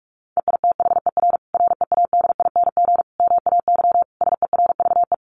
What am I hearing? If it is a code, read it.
Morse code, heard as a keyed tone, about 44 wpm, 700 Hz. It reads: EITHER READING MAY SERVE
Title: Sentences from Top 1400 Words 45wpm